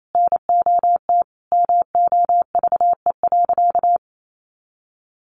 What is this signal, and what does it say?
Morse code, heard as a keyed tone, about 28 wpm, 700 Hz. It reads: NOT MOVE.
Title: A Sound Of Thunder 28wpm